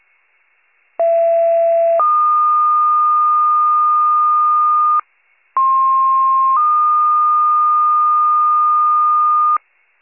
Two-tone page, heard as a one-second tone, then a three-second tone, about 670 then 1220 Hz, about 1060 then 1220 Hz. It is heard from 1.0 to 9.6 s.